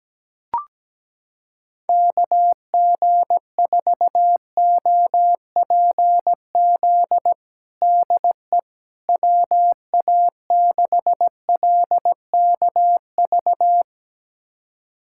Morse code, heard as a keyed tone, about 17 wpm, 700 Hz. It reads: KG4OPZ DE WA6LKV
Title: USA Callsign DE Callsign - 6 Characters - 17wpm